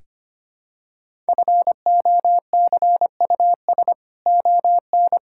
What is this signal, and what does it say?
Morse code, heard as a keyed tone, about 25 wpm, 700 Hz. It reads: FOCUS ON
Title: Sentences from Top 1000 Words - Review Part 6 - 25wpm